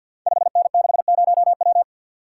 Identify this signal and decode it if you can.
Morse code, heard as a keyed tone, about 51 wpm, 700 Hz. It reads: 5N60W
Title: Code Groups - 5 Letter:Numbers - 50wpm